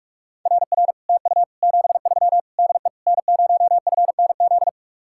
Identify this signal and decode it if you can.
Morse code, heard as a keyed tone, about 58 wpm, 700 Hz. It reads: RR TU 73 DE N0FNZ